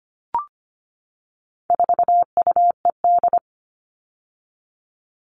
Morse code, heard as a keyed tone, about 25 wpm, 700 Hz. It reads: 4UED
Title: All Cut Numbers - 4-Digit 25wpm